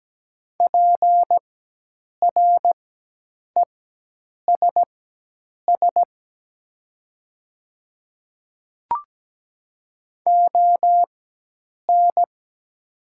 Morse code, heard as a keyed tone, about 17 wpm, spaced as for 8 wpm, 700 Hz. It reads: PRESS ON
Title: 4x Character Spacing - Top 300 Words - 17wpm